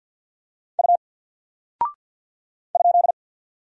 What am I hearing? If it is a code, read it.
Morse code, heard as a keyed tone, about 50 wpm, 700 Hz. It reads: U ?
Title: Single Character (Letter:Number:Punctuation:Prosign) 50wpm